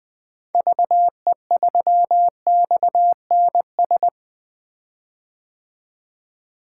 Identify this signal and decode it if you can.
Morse code, heard as a keyed tone, about 20 wpm, 700 Hz. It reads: VE3XNS
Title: 2023 POTA Activator Call Signs - 20wpm